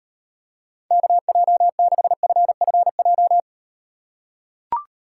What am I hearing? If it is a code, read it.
Morse code, heard as a keyed tone, about 38 wpm, 700 Hz. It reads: KJ6FFJ